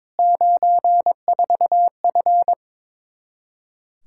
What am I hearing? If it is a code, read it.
Morse code, heard as a keyed tone, about 22 wpm, 700 Hz. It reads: 94F
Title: Practice QSO - Exchange 2 - Info - 22wpm